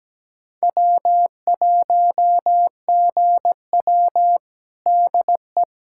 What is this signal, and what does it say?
Morse code, heard as a keyed tone, about 17 wpm, 700 Hz. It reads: W1GW DE
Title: USA Callsign DE Callsign - 4 Characters - 17wpm